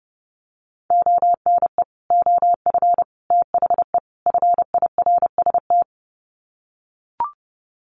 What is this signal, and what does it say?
Morse code, heard as a keyed tone, about 28 wpm, 700 Hz. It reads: ONE OF THE FIRST
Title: Sets of 4 Words 30wpm